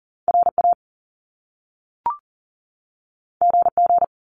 Morse code, heard as a keyed tone, about 40 wpm, 700 Hz. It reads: RA GG
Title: Top Two Letter Combinations in Words 40wpm